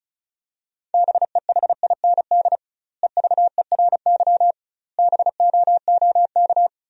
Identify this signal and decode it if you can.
Morse code, heard as a keyed tone, about 34 wpm, 700 Hz. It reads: BEHIND EVERY BOOK
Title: 2x Word-Spacing - Sentences from Top 500 Words - 35wpm